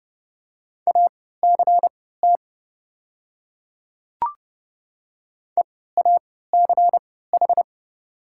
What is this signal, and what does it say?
Morse code, heard as a keyed tone, about 31 wpm, spaced as for 17 wpm, 690 Hz. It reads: ACT EACH
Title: 3x Character Spacing - Top 200 Words - 30wpm